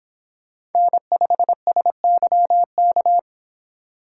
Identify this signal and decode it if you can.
Morse code, heard as a keyed tone, about 26 wpm, 700 Hz. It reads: N5SYK